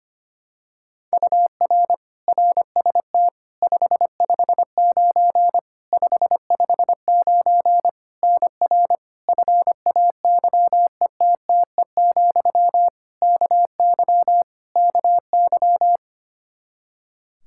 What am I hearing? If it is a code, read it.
Morse code, heard as a keyed tone, about 25 wpm, 700 Hz. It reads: UR RST 559 559 NR FAYETTE, KY KY
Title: Practice QSO - Exchange 1 - Beginning - 25wpm